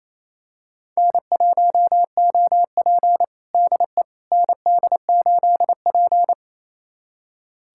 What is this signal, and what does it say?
Morse code, heard as a keyed tone, about 28 wpm, 700 Hz. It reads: N1OP DE ND8P
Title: USA Callsign DE Callsign - 4 Characters - 28wpm